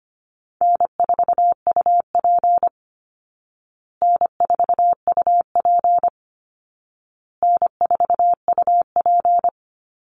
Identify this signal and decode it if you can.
Morse code, heard as a keyed tone, about 25 wpm, 700 Hz. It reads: N4UP N4UP N4UP
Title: Call Signs Repeated 2x - 25wpm